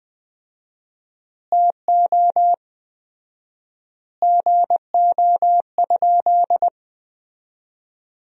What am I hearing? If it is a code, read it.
Morse code, heard as a keyed tone, about 21 wpm, 700 Hz. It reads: TO GO?
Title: 4x Word-Spacing - Sentences from Top 300 Words - 20wpm